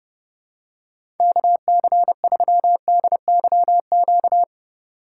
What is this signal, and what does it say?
Morse code, heard as a keyed tone, about 30 wpm, 700 Hz. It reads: KC3DYQ
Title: USA Call Signs - 6 Characters - 3x Character Spacing - 30wpm